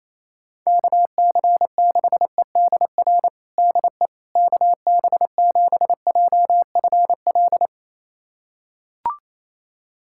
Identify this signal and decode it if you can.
Morse code, heard as a keyed tone, about 28 wpm, 700 Hz. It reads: KC6EDR DE KB7JFL
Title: USA Callsign DE Callsign 28wpm